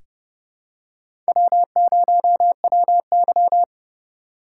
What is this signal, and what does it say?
Morse code, heard as a keyed tone, about 30 wpm, 700 Hz. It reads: W0WY
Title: USA Call Signs - 4 Characters - 8x Character Spacing - 30wpm